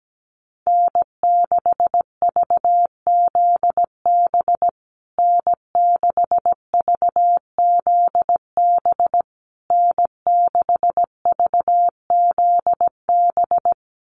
Morse code, heard as a keyed tone, about 17 wpm, 700 Hz. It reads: N6VZB N6VZB N6VZB
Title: USA Call Signs Repeated 3x 17wpm